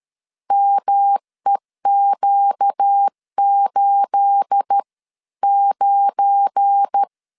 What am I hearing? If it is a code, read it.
Morse code, heard as a keyed tone, about 13 wpm, spaced as for 11 wpm, 800 Hz. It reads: MEQ89
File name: CW / DET / 1 9 1500C S B 12wpm 3819